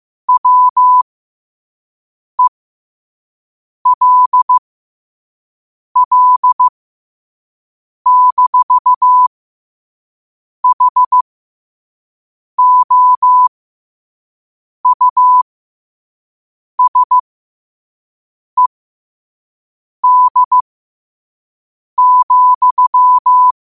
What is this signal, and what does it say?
Morse code, heard as a keyed tone, about 15 wpm, spaced as for 5 wpm, 1000 Hz. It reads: WELL-HOUSED,
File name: qotdfiles / 5wpm